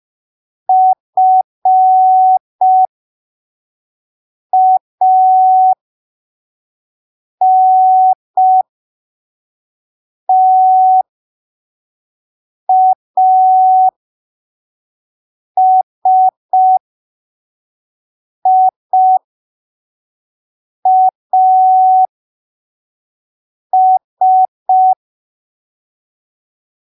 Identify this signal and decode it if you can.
Morse code, heard as a keyed tone, about 5 wpm, spaced as for 3 wpm, 750 Hz. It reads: FANTASIAS
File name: arquivos / texto / texto 26 05ppm